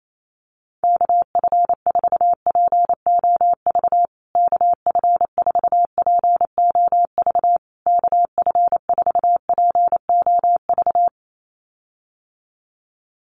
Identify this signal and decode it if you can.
Morse code, heard as a keyed tone, about 28 wpm, 700 Hz. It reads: KF4POV KF4POV KF4POV
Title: USA Call Signs Repeated 3x 28wpm